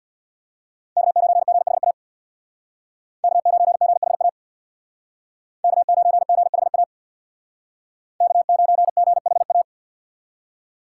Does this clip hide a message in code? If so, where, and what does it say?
Yes: Morse code from 1.0 to 9.6 s, heard as a keyed tone, about 67 wpm, 700 Hz. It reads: K9GHA K9GHA K9GHA K9GHA